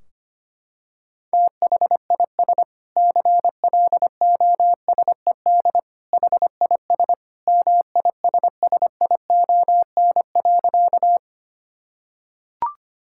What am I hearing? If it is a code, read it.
Morse code, heard as a keyed tone, about 25 wpm, 700 Hz. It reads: THIS CLOSED HIS MISSION.
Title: Sentences from Top 1600 Words 25wpm